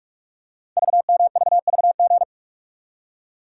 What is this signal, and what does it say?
Morse code, heard as a keyed tone, about 44 wpm, 690 Hz. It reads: VMVVG
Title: Code Groups - 5 Letters - 45wpm